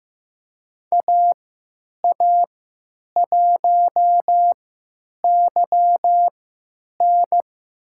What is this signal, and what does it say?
Morse code, heard as a keyed tone, about 15 wpm, spaced as for 9 wpm, 700 Hz. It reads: AA1YN
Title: USA Call Signs - 5 Characters - 3x Character Spacing - 15wpm